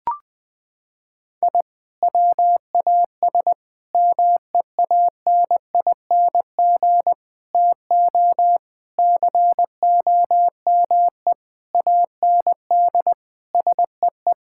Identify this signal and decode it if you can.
Morse code, heard as a keyed tone, about 20 wpm, 700 Hz. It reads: I WAS MEANING TO COME AND SEE
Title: Sentences from Top 1000 Words - Review Part 6 - 20wpm